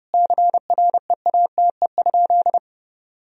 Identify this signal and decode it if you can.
Morse code, heard as a keyed tone, about 32 wpm, 700 Hz. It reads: CREATE?
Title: Sentences from Top 1000 Words - Review Part 3 - 30wpm